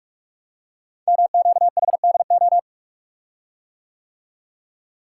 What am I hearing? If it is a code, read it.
Morse code, heard as a keyed tone, about 44 wpm, 690 Hz. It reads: MQHDO
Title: Code Groups - 5 Letters - 45wpm